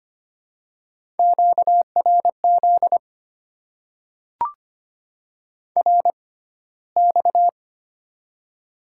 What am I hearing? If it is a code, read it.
Morse code, heard as a keyed tone, about 25 wpm, spaced as for 15 wpm, 700 Hz. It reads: QRZ RX
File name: Top 90 QSO Elements - 6x Character Spacing - 25wpm